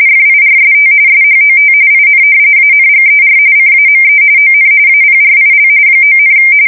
RTTY, about 45.45 baud. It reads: FIRST SHOT IN ANGER CAME ON APRIL 9,
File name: fsk170 / usnavy4